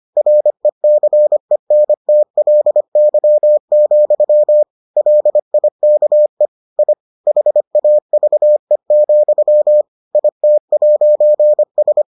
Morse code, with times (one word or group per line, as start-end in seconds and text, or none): 0.2-4.6 RECENTLY,
5.0-6.5 LIKE
6.8-6.9 I
7.3-9.8 HAVE,
10.1-12.0 IT'S